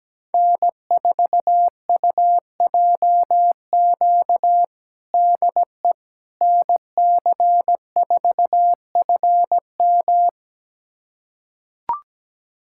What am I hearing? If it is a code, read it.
Morse code, heard as a keyed tone, about 17 wpm, 700 Hz. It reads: N4UJQ DE NC4FM
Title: USA Callsign DE Callsign - 5 Characters - 17wpm